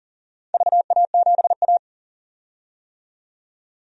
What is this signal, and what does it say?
Morse code, heard as a keyed tone, about 44 wpm, 700 Hz. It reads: VA7A
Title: DXCC-100 Call Signs - 4 Characters - 40wpm